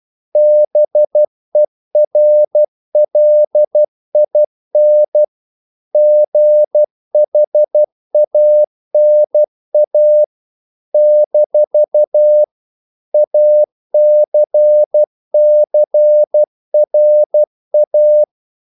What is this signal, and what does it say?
Morse code, heard as a keyed tone, about 12 wpm, 600 Hz. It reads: BERLIN GHANA - ACCRA
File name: CW / Practice / NationalCapitals12